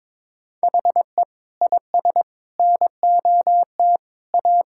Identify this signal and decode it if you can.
Morse code, heard as a keyed tone, about 22 wpm, 700 Hz. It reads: HE IS NOT A
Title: Sentences from Top 1600 Words 22wpm